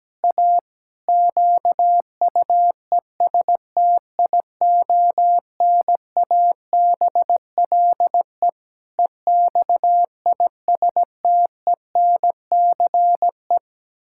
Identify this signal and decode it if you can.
Morse code, heard as a keyed tone, about 17 wpm, 700 Hz. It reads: A QUESTIONABLE EXISTENCE
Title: Pangram Challenge v2 - 17wpm